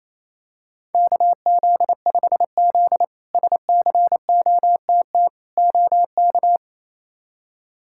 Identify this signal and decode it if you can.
Morse code, heard as a keyed tone, about 28 wpm, 700 Hz. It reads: KZ5Z SCOTT OK
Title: K1USN SST QSO Practice - 28wpm